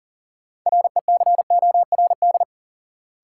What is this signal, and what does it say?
Morse code, heard as a keyed tone, about 40 wpm, 700 Hz. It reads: RECORD
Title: Sentences from Top 500 Words - New Word Drill - 40wpm